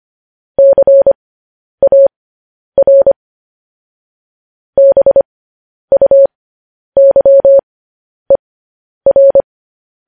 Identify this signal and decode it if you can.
Morse code, heard as a keyed tone, about 25 wpm, spaced as for 10 wpm, 550 Hz. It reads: CAR BUYER
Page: news-headlines-in-morse-code-at-10-wpm